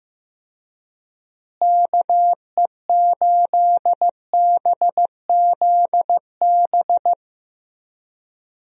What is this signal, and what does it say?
Morse code, heard as a keyed tone, about 15 wpm, 700 Hz. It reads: KE8BZB
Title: USA Call Signs - 6 Characters - Repeated 3x - 15wpm